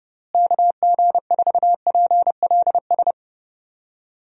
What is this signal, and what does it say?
Morse code, heard as a keyed tone, about 30 wpm, 700 Hz. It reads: KG4PLS
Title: USA Call Signs - 6 Characters - Repeated 3x - 30wpm